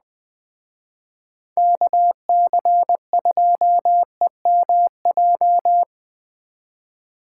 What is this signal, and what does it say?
Morse code, heard as a keyed tone, about 20 wpm, 700 Hz. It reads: KC2EMJ